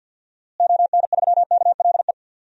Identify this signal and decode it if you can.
Morse code, heard as a keyed tone, about 53 wpm, 700 Hz. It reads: ON3KLE
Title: DXCC-100 Call Signs - 6 Characters - Repeated 2x - 50wpm